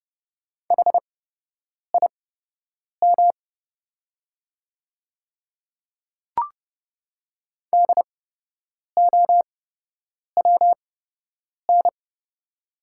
Morse code, heard as a keyed tone, about 30 wpm, spaced as for 8 wpm, 690 Hz. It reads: HIM DOWN
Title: 8x Character Spacing - Top 100 Words - 30wpm